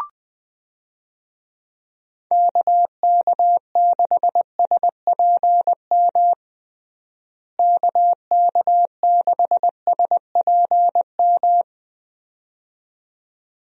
Morse code, heard as a keyed tone, about 20 wpm, 700 Hz. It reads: KK6SPM KK6SPM
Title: USA Call Signs - 6 Characters - Repeated 2x - 20wpm